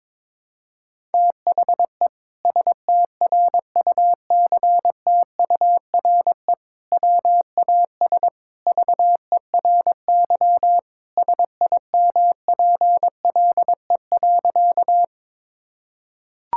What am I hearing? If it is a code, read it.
Morse code, heard as a keyed tone, about 22 wpm, 700 Hz. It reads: THE STRUCTURE WAS VERY SIMPLE.
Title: Sentences from Top 1000 Words - Review Part 6 - 22wpm